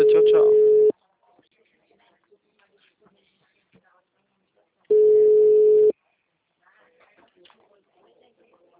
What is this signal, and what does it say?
Ringback tone (Europe, 425 Hz), with speech.